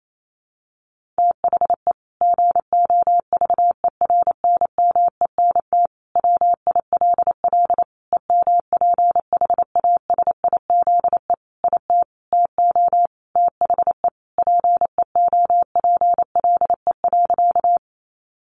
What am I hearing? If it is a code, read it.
Morse code, heard as a keyed tone, about 28 wpm, 700 Hz. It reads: THE GOVERNMENT WILL EMPHASIZE IT TO THE PEOPLE.
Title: Sentences from Top 1500 Words 28wpm